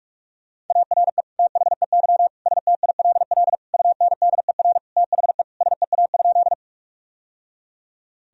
Morse code, heard as a keyed tone, about 46 wpm, 700 Hz. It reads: ARE THEY STILL UNDER THE SEA?
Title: Sentences from Top 1000 Words - Review Part 1 - 45wpm